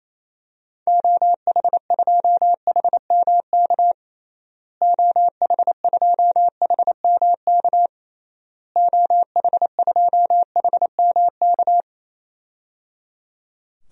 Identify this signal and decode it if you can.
Morse code, heard as a keyed tone, about 28 wpm, 700 Hz. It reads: OH2HMK OH2HMK OH2HMK
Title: DXCC-100 Call Signs - 6 Characters - Repeated 3x - 28wpm